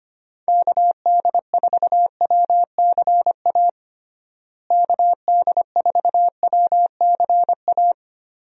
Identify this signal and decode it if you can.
Morse code, heard as a keyed tone, about 25 wpm, 700 Hz. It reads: KD4WCA KD4WCA
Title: USA Call Signs - 6 Characters - Repeated 2x - 25wpm